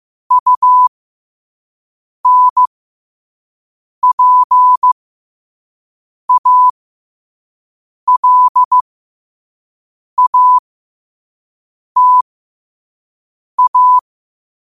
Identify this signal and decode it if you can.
Morse code, heard as a keyed tone, about 15 wpm, spaced as for 5 wpm, 1000 Hz. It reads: UNPALATA
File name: 5wpm